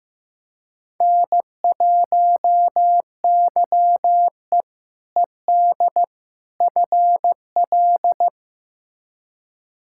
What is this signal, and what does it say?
Morse code, heard as a keyed tone, about 15 wpm, 700 Hz. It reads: N1YE ED FL
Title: K1USN SST QSO Practice - 15wpm